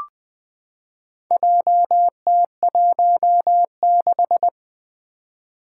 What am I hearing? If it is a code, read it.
Morse code, heard as a keyed tone, about 21 wpm, 700 Hz. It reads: JT16
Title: Four Challenging Letter-Number Combinations 20wpm